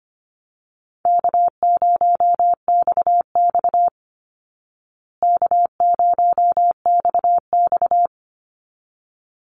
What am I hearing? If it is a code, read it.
Morse code, heard as a keyed tone, about 25 wpm, 700 Hz. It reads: K0XX K0XX
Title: Easy Call Signs Repeated 2x - 25wpm